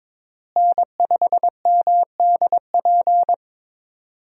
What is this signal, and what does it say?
Morse code, heard as a keyed tone, about 22 wpm, 700 Hz. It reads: N5MDP
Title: USA Call Signs - 5 Characters - Repeated 3x - 22wpm